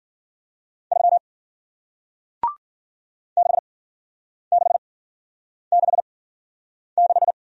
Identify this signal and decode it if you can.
Morse code, heard as a keyed tone, about 55 wpm, spaced as for 9 wpm, 700 Hz. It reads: 3 6666